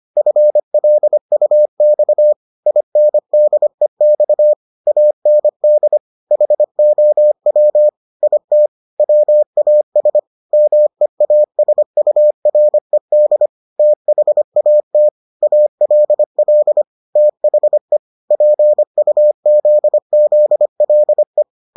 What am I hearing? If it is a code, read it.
Morse code, heard as a keyed tone, about 25 wpm, 600 Hz. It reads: FLUX INDEX AND HOW IT WAS MEASURED THAT ALL THE PUZZLE